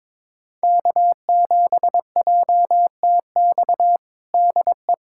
Morse code, heard as a keyed tone, about 22 wpm, 700 Hz. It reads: K7JTX DE
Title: USA Callsign DE Callsign - 5 Characters - 22wpm